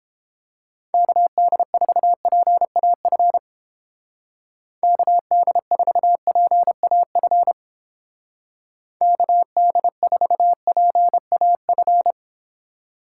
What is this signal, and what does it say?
Morse code, heard as a keyed tone, about 29 wpm, 700 Hz. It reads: KD4PAF KD4PAF KD4PAF